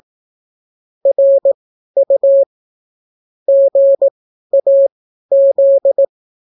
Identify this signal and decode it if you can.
Morse code, heard as a keyed tone, about 18 wpm, spaced as for 12 wpm, 550 Hz. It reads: RU GAZ